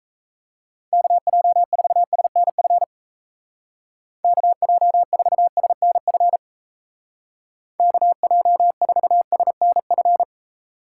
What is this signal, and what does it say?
Morse code, heard as a keyed tone, about 35 wpm, 700 Hz. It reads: KJ4SNF KJ4SNF KJ4SNF